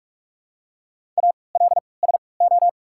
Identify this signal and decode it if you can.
Morse code, heard as a keyed tone, about 48 wpm, spaced as for 26 wpm, 700 Hz. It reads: ALSO
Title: 3x Character Spacing - Top 200 Words - 45wpm